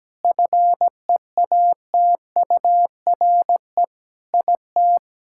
Morse code, heard as a keyed tone, about 17 wpm, 700 Hz. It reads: FEATURE IT
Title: Sentences from Top 1000 Words - Review Part 5 - 17wpm